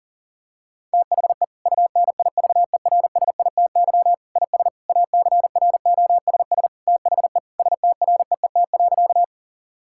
Morse code, heard as a keyed tone, about 40 wpm, 700 Hz. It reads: THE UNIVERSITY IS ACROSS THE STREET.
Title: Sentences from Top 1000 Words - Review Part 5 - 40wpm